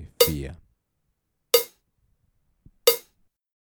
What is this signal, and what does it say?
Speech. There is no code.